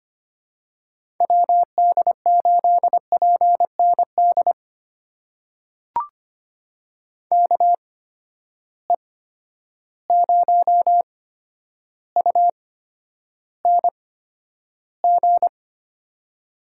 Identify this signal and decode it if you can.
Morse code, heard as a keyed tone, about 25 wpm, spaced as for 11 wpm, 700 Hz. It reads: WD8PND KE0UNG
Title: USA Call Signs - 6 Characters - 8x Character Spacing - 25wpm